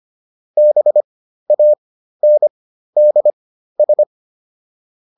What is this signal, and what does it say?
Morse code, heard as a keyed tone, about 25 wpm, spaced as for 13 wpm, 600 Hz. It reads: BANDS